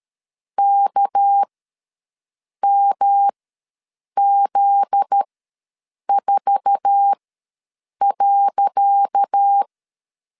Morse code, heard as a keyed tone, about 13 wpm, spaced as for 7 wpm, 800 Hz. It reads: KMZ4.
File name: LEZIONI / 10 SIGNS / 10C DOT ws 12wpm 3548